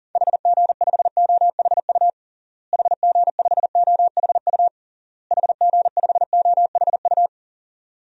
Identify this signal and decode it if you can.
Morse code, heard as a keyed tone, about 40 wpm, 700 Hz. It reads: HG5OHU HG5OHU HG5OHU
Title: DXCC-100 Call Signs - 6 Characters - Repeated 3x - 40wpm